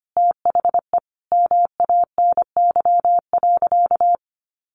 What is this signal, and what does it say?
Morse code, heard as a keyed tone, about 25 wpm, 700 Hz. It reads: THE MANY.
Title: Sentences from Top 1300 Words 25wpm